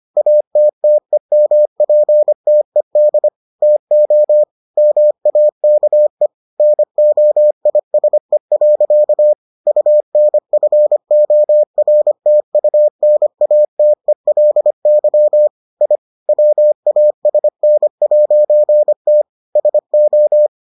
Morse code, heard as a keyed tone, about 25 wpm, 600 Hz. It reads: ATTEMPTED TO MAKE NOISE. UNFORTUNATELY I WASN'T SO